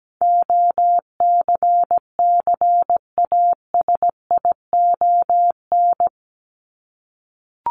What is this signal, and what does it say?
Morse code, heard as a keyed tone, about 17 wpm, 700 Hz. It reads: OCCASION
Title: Top 1501-1750 Words 17wpm